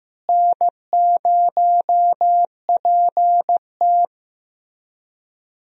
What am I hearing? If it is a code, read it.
Morse code, heard as a keyed tone, about 15 wpm, 700 Hz. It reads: N0PT
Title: USA Call Signs - 4 Characters - Repeated 3x - 15wpm